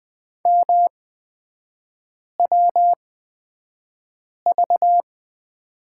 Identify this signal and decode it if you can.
Morse code, heard as a keyed tone, about 20 wpm, spaced as for 5 wpm, 700 Hz. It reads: MWV
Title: 250 Lesson 40e - BK - Three Characters 20wpm